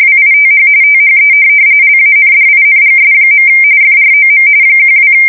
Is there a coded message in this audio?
Yes: RTTY from 0.0 to 5.1 s, about 45.45 baud. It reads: WITH THE LANDINGS. FROM 12